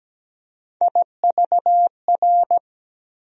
Morse code, heard as a keyed tone, about 17 wpm, 700 Hz. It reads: IVR
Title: USA Call Sign Postfixes - 3 Letters 17wpm